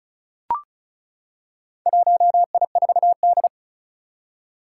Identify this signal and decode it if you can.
Morse code, heard as a keyed tone, about 36 wpm, 700 Hz. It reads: 1I4D
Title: Four Challenging Letter-Number Combinations 35wpm